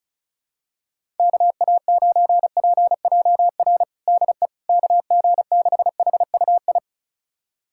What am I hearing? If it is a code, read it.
Morse code, heard as a keyed tone, about 36 wpm, 700 Hz. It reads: KA9PJR DE KG6HUI
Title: USA Callsign DE Callsign 35wpm